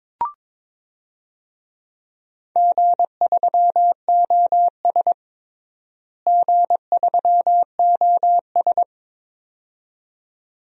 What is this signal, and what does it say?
Morse code, heard as a keyed tone, about 22 wpm, 700 Hz. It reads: G3OS G3OS
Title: DXCC-100 Call Signs - 4 Characters - Repeated 2x - 22wpm